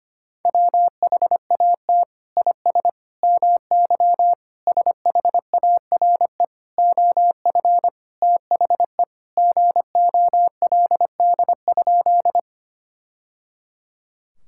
Morse code, heard as a keyed tone, about 25 wpm, 700 Hz. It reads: WHAT IS MY SHARE OF THE GOLD?
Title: Sentences from Top 1000 Words - Review Part 6 - 25wpm